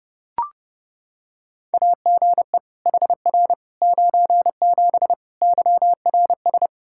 Morse code, heard as a keyed tone, about 32 wpm, 700 Hz. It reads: AGE HR 97 YRS
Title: Practice QSO - Exchange 3 - More Info - 30wpm